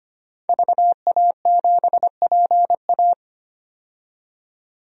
Morse code, heard as a keyed tone, about 25 wpm, 700 Hz. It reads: VA7PA
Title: DXCC-100 Call Signs - 5 Characters - 25wpm